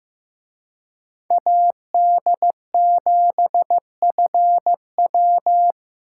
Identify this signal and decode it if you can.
Morse code, heard as a keyed tone, about 18 wpm, 700 Hz. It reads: AD7FW
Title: Speed Racing - USA Call Signs 15wpm